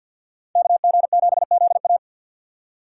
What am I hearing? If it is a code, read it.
Morse code, heard as a keyed tone, about 62 wpm, 700 Hz. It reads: KG7ZA
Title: Speed Racing - USA Call Signs 50wpm